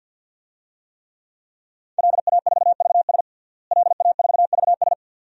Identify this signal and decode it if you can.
Morse code, heard as a keyed tone, about 49 wpm, 700 Hz. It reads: LA4VS LA4VS
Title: DXCC-100 Call Signs - 5 Characters - Repeated 2x - 50wpm